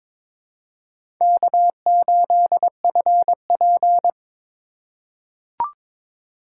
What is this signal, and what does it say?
Morse code, heard as a keyed tone, about 22 wpm, 700 Hz. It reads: K8FP
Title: USA Call Signs - 4 Characters - 2x Character Spacing - 22wpm